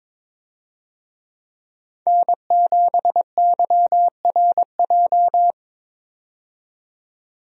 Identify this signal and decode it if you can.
Morse code, heard as a keyed tone, about 21 wpm, 700 Hz. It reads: N7YRJ
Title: USA Call Signs - 5 Characters - 22wpm